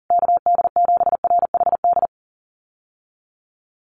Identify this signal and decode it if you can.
Morse code, heard as a keyed tone, about 40 wpm, 700 Hz. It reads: KD7RHD
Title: USA Call Signs Repeated 3x 40wpm